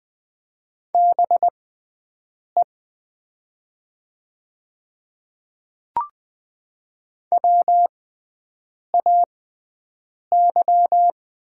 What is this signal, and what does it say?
Morse code, heard as a keyed tone, about 20 wpm, spaced as for 7 wpm, 700 Hz. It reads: BE WAY
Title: 6x Character Spacing - Top 200 Words - 20wpm